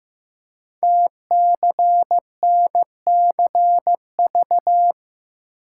Morse code, heard as a keyed tone, about 15 wpm, 700 Hz. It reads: TCNCV
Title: Code Groups - 5 Letters - 15wpm